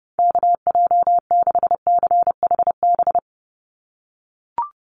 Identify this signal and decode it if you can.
Morse code, heard as a keyed tone, about 30 wpm, 700 Hz. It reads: KJ6CHB
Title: USA Call Signs 30wpm